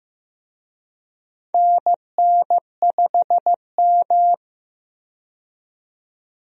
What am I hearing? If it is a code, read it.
Morse code, heard as a keyed tone, about 15 wpm, 700 Hz. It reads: NN5M
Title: USA Call Signs - 4 Characters - 15wpm